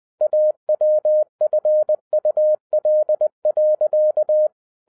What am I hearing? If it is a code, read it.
Morse code, heard as a keyed tone, about 20 wpm, 610 Hz. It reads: AWFUL.
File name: 2011-07-mp3 / n58 BACK FROM A BIG ACCIDENT 20wpm